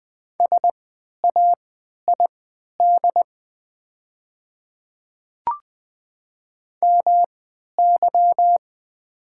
Morse code, heard as a keyed tone, about 20 wpm, spaced as for 11 wpm, 700 Hz. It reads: SAID MY
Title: 3x Character Spacing - Top 100 Words - 20wpm